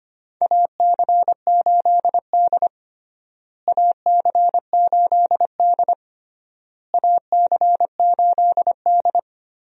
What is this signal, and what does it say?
Morse code, heard as a keyed tone, about 25 wpm, 700 Hz. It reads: AC8D AC8D AC8D
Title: USA Call Signs - 4 Characters - Repeated 3x - 25wpm